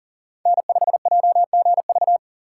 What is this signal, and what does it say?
Morse code, heard as a keyed tone, about 40 wpm, 700 Hz. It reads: N5JGV
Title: USA Call Signs - 5 Characters - 2x Character Spacing - 40wpm